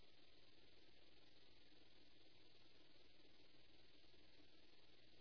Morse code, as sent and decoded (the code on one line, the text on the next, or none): none